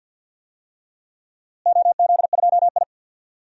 Morse code, heard as a keyed tone, about 48 wpm, 700 Hz. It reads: OZ2I
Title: DXCC-100 Call Signs - 4 Characters - Repeated 3x - 50wpm